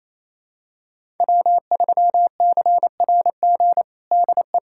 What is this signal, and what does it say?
Morse code, heard as a keyed tone, about 28 wpm, 700 Hz. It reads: W3CRG DE
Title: USA Callsign DE Callsign - 5 Characters - 28wpm